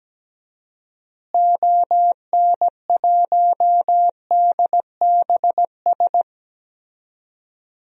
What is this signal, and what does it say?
Morse code, heard as a keyed tone, about 17 wpm, 700 Hz. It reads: ON1DBS
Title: DXCC-100 Call Signs - 6 Characters - Repeated 2x - 17wpm